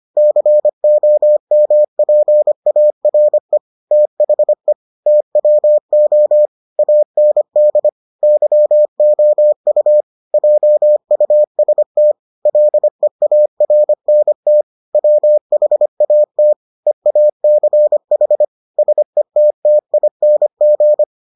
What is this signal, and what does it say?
Morse code, heard as a keyed tone, about 25 wpm, 600 Hz. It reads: COMPARE THE TWO AND YOU JUST LEARNT WHAT EACH SETTING